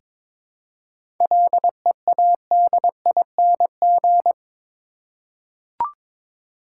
Morse code, heard as a keyed tone, about 22 wpm, 700 Hz. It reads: LEADING
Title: Seven Letter Words - 22wpm